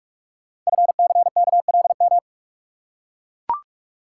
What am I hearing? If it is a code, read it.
Morse code, heard as a keyed tone, about 44 wpm, 690 Hz. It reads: FCKLM
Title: Code Groups - 5 Letters - 45wpm